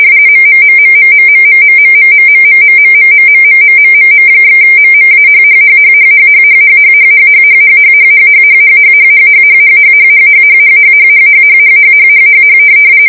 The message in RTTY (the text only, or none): MMMMMMMMMMMMMMMMMMMMMMMM/MMAHHHHHHHHHHHHHHHHHHHHHHHHHHHHHHHHHHHHHHHHHHHH